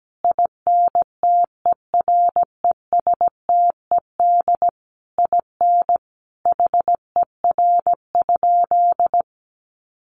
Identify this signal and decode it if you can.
Morse code, heard as a keyed tone, about 17 wpm, 700 Hz. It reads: INTERESTED IN HER?
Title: Sentences from Top 900 Words 17wpm